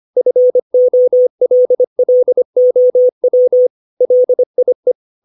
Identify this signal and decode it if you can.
Morse code, heard as a keyed tone, about 25 wpm, 500 Hz. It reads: FOLLOW LIE
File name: 632 words / 632 Words 25WPM